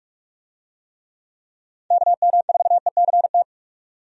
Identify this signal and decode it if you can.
Morse code, heard as a keyed tone, about 46 wpm, 700 Hz. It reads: KM4ECT
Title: USA Call Signs - 6 Characters - 45wpm